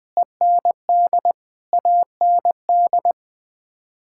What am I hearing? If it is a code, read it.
Morse code, heard as a keyed tone, about 21 wpm, 700 Hz. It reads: END AND
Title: Sets of 9 Words - 20wpm